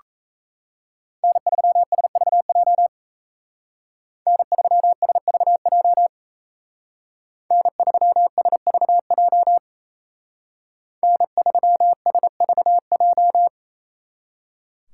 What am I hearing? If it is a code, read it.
Morse code, heard as a keyed tone, about 34 wpm, 700 Hz. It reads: N3SVJ N3SVJ N3SVJ N3SVJ
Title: Speed Racing - USA 1x3 Call Signs 28wpm